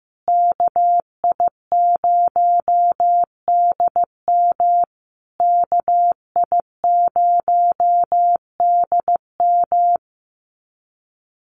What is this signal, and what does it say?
Morse code, heard as a keyed tone, about 15 wpm, 700 Hz. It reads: KI0DM KI0DM
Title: USA Call Signs Repeated 3x 15wpm